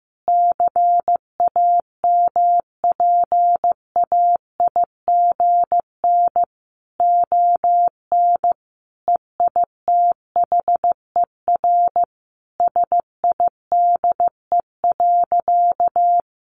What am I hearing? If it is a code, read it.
Morse code, heard as a keyed tone, about 15 wpm, 700 Hz. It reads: CAMPAIGN ON EITHER SIDE.
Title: Sentences from Top 1200 Words 15wpm